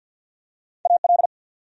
Morse code, heard as a keyed tone, about 52 wpm, 700 Hz. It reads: AL
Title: USA Common Names - Max 3 Letters - 50wpm